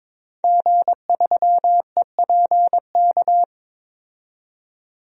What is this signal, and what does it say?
Morse code, heard as a keyed tone, about 22 wpm, 700 Hz. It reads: G3EPK
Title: DXCC-100 Call Signs - 5 Characters - 22wpm